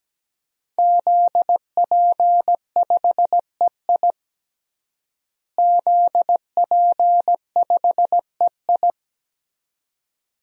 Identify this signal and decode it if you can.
Morse code, heard as a keyed tone, about 17 wpm, 700 Hz. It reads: ZP5EI ZP5EI
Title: DXCC-100 Call Signs - 5 Characters - Repeated 2x - 17wpm